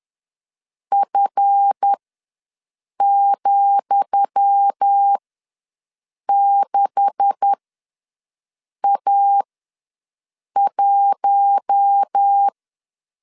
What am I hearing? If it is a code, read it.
Morse code, heard as a keyed tone, about 10 wpm, spaced as for 5 wpm, 800 Hz. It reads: F,6A1
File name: LEZIONI / 10 SIGNS / 10D COMA ws 10wpm 4314